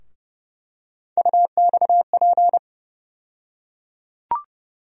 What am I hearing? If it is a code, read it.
Morse code, heard as a keyed tone, about 30 wpm, 700 Hz. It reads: UXP
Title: USA Call Sign Postfixes - 3 Letters 30wpm